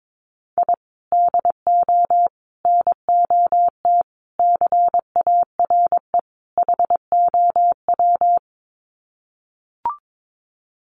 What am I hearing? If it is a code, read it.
Morse code, heard as a keyed tone, about 22 wpm, 700 Hz. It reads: I DO NOT CARE HOW
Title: Sets of 5 Words Encore2 22wpm